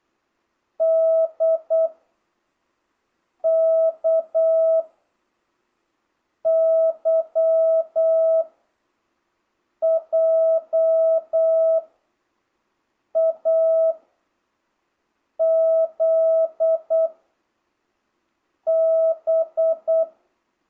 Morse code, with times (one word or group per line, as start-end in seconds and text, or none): 0.8-1.9 D
3.4-4.8 K
6.5-8.4 Y
9.8-11.8 J
13.2-13.9 A
15.4-17.1 Z
18.7-20.0 B